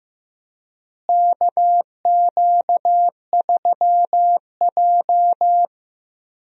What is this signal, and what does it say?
Morse code, heard as a keyed tone, about 15 wpm, 700 Hz. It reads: KQ3J
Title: USA Call Signs - 4 Characters - Repeated 3x - 15wpm